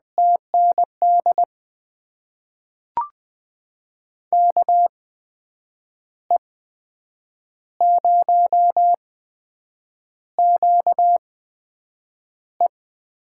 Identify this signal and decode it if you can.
Morse code, heard as a keyed tone, about 20 wpm, spaced as for 7 wpm, 700 Hz. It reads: TND KE0QE
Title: USA Call Signs - 6 Characters - 8x Character Spacing - 20wpm